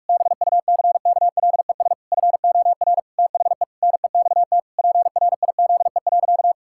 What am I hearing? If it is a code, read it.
Morse code, heard as a keyed tone, about 46 wpm, 700 Hz. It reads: BUCKLES FOR THE NEXT PRIZE.
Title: Pangram Challenge v2 - 45wpm